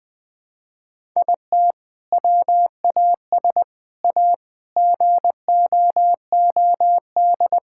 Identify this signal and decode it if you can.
Morse code, heard as a keyed tone, about 20 wpm, 700 Hz. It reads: IT WAS A GOOD
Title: Sentences with 2-Word Phrases - 20wpm